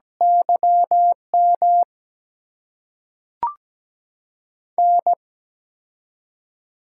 Morse code, heard as a keyed tone, about 17 wpm, spaced as for 5 wpm, 700 Hz. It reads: YMN
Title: USA Call Signs - 8x Character Spacing - 17wpm